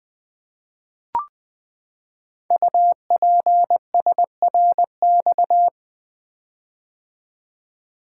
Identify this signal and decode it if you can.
Morse code, heard as a keyed tone, about 20 wpm, 700 Hz. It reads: UPSRX
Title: Code Groups - 5 Letters - 20wpm